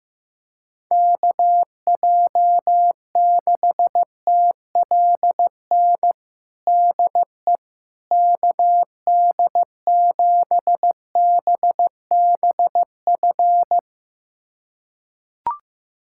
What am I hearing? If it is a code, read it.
Morse code, heard as a keyed tone, about 16 wpm, 700 Hz. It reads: KJ6TLN DE KD7BBF
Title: USA Callsign DE Callsign - 6 Characters - 15wpm